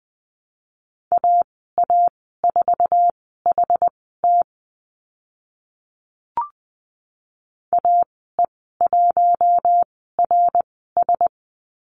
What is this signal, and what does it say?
Morse code, heard as a keyed tone, about 20 wpm, spaced as for 15 wpm, 700 Hz. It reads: AA4HT AE1RS
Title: 076 Lesson 13g - Letter L - Call Signs 20wpm